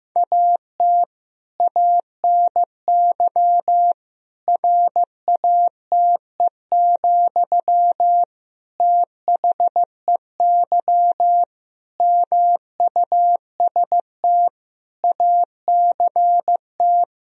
Morse code, heard as a keyed tone, about 15 wpm, 700 Hz. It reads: AT ANY RATE, THEY MUST ACT
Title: Sentences from Top 1000 Words - Review Part 4 - 15wpm